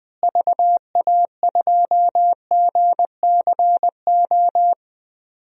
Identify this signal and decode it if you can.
Morse code, heard as a keyed tone, about 20 wpm, 700 Hz. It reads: VA2GCO
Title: DXCC-100 Call Signs - 6 Characters - Repeated 2x - 20wpm